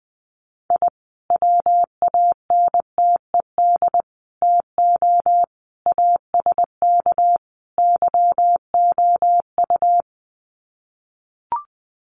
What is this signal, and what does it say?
Morse code, heard as a keyed tone, about 20 wpm, 700 Hz. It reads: I WANTED TO ASK YOU
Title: Sets of 5 Words Encore2 20wpm